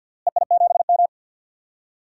Morse code, heard as a keyed tone, about 52 wpm, 700 Hz. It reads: EI7M
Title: DXCC-100 Call Signs - 4 Characters - Repeated 2x - 50wpm